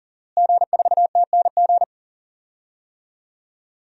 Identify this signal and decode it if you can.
Morse code, heard as a keyed tone, about 40 wpm, 700 Hz. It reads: G4TNG
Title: DXCC-100 Call Signs - 5 Characters - Repeated 2x - 40wpm